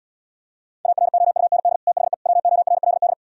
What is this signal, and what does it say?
Morse code, heard as a keyed tone, about 77 wpm, 700 Hz. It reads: AROUND THE WORLD